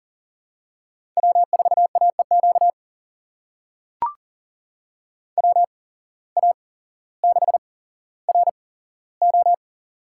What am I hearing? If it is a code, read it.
Morse code, heard as a keyed tone, about 39 wpm, spaced as for 17 wpm, 700 Hz. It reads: W4AEQ WA6RO
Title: USA Call Signs - 5 Characters - 8x Character Spacing - 40wpm